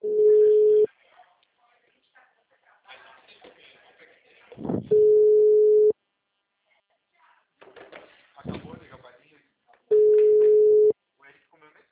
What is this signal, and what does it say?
Ringback tone (Europe, 425 Hz), with speech.